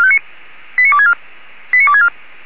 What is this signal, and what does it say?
Five-tone selective call (ZVEI), heard as quick runs of five tones, tones of about 70 ms. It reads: 89264 89264